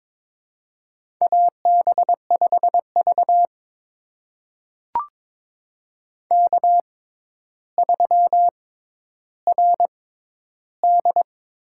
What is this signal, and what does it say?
Morse code, heard as a keyed tone, about 22 wpm, spaced as for 11 wpm, 700 Hz. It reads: AB5V K3RD
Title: USA Call Signs - 4 Characters - 6x Character Spacing - 22wpm